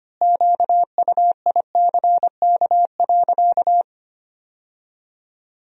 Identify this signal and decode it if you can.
Morse code, heard as a keyed tone, about 25 wpm, 700 Hz. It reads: QUICK.
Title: 2x Word-Spacing - Sentences from Top 500 Words - 25wpm